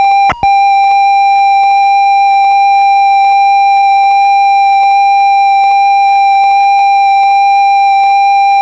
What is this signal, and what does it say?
Computer data on cassette.